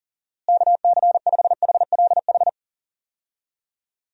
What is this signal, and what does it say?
Morse code, heard as a keyed tone, about 40 wpm, 700 Hz. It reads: KC5HLH
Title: USA Call Signs - 6 Characters - Repeated 2x - 40wpm